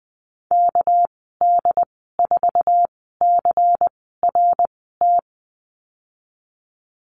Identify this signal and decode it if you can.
Morse code, heard as a keyed tone, about 20 wpm, spaced as for 14 wpm, 700 Hz. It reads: KD4CRT
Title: 245 Lesson 39g - Letter z - Call Signs 20wpm